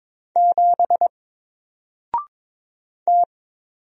Morse code, heard as a keyed tone, about 22 wpm, 700 Hz. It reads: 7 T
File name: Single-Letter-Number-22wpm